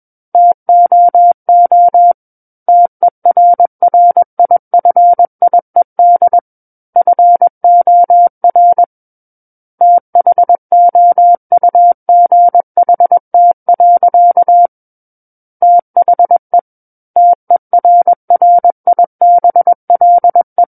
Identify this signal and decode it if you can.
Morse code, heard as a keyed tone, about 21 wpm, 700 Hz. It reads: TOO TERRIFIED FOR THOUGHT. THE TERRIBLE
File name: Book One / 12-Chapter 12 - 21 WPM